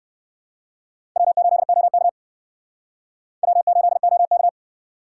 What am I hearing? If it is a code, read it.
Morse code, heard as a keyed tone, about 58 wpm, 700 Hz. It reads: W8OK W8OK